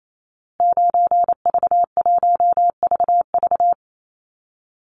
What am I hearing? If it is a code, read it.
Morse code, heard as a keyed tone, about 28 wpm, 700 Hz. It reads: 9V1VV
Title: Call Signs Repeated 2x - 28wpm